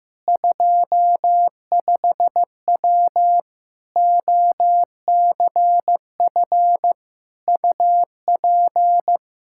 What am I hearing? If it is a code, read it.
Morse code, heard as a keyed tone, about 16 wpm, 700 Hz. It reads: 25W OCF UP
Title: Practice QSO - Exchange 2 - Info - 15wpm